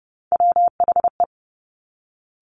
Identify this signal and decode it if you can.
Morse code, heard as a keyed tone, about 30 wpm, 690 Hz. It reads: WHE
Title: Top Three Letter Combinations in Words 30wpm